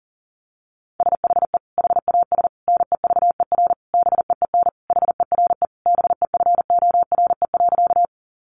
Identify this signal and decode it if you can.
Morse code, heard as a keyed tone, about 40 wpm, 700 Hz. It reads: SHE HAS NEVER BEEN HERE BEFORE.
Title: Additional Easy Phrases 40wpm